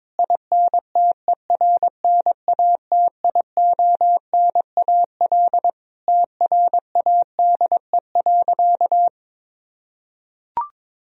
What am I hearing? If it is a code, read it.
Morse code, heard as a keyed tone, about 22 wpm, 700 Hz. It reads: INTERNATIONAL TRADE.
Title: Sentences from Top 1000 Words - Review Part 4 - 22wpm